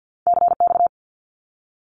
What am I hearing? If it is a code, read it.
Morse code, heard as a keyed tone, about 51 wpm, 700 Hz. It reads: CX
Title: Abbreviations Repeated 3x - 50wpm